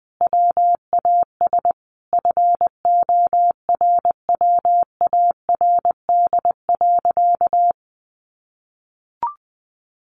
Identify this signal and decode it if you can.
Morse code, heard as a keyed tone, about 20 wpm, 700 Hz. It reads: WAS FORWARD.
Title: Sentences from Top 1500 Words 20wpm